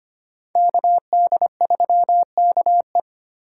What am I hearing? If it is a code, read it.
Morse code, heard as a keyed tone, about 32 wpm, 700 Hz. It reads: KD3KE